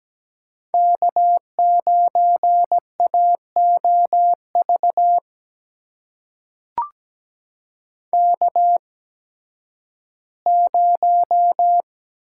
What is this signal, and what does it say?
Morse code, heard as a keyed tone, about 17 wpm, spaced as for 11 wpm, 700 Hz. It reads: K9AOV K0
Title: USA Call Signs - 5 Characters - 8x Character Spacing - 17wpm